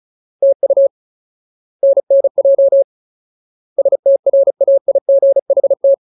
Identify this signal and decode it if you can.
Morse code, heard as a keyed tone, about 34 wpm, 550 Hz. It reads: TU NNJ STRAIGHT
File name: Ultimate Word List / 35WPM Ultimate